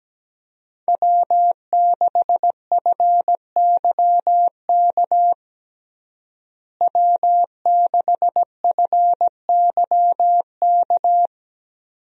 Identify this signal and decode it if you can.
Morse code, heard as a keyed tone, about 17 wpm, 700 Hz. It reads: W6FYK W6FYK
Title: USA Call Signs - 5 Characters - Repeated 2x - 17wpm